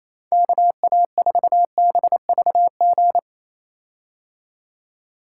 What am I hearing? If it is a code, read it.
Morse code, heard as a keyed tone, about 28 wpm, 700 Hz. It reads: KA4BVG
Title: K1USN SST QSO Practice - 28wpm